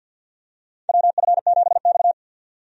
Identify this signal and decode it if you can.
Morse code, heard as a keyed tone, about 50 wpm, 700 Hz. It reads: WF7X
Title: USA Call Signs - 4 Characters - 50wpm